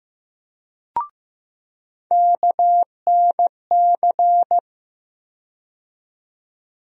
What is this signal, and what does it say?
Morse code, heard as a keyed tone, about 15 wpm, 700 Hz. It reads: KNC
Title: USA Call Sign Postfixes - 3 Letters 15wpm